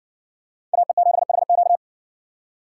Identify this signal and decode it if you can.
Morse code, heard as a keyed tone, about 60 wpm, 700 Hz. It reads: AE7HQ